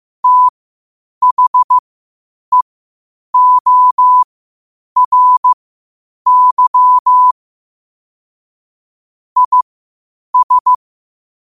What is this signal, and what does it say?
Morse code, heard as a keyed tone, about 15 wpm, spaced as for 8 wpm, 1000 Hz. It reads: THEORY IS
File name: qotdfiles / 8wpm